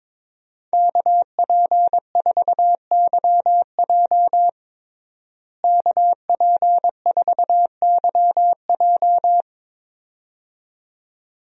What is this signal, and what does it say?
Morse code, heard as a keyed tone, about 21 wpm, 700 Hz. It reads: KP4YJ KP4YJ
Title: DXCC-100 Call Signs - 5 Characters - Repeated 2x - 22wpm